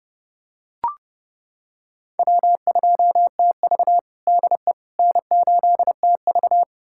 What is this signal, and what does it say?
Morse code, heard as a keyed tone, about 30 wpm, 700 Hz. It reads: W2TV DE N8TV
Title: USA Callsign DE Callsign - 4 Characters - 30wpm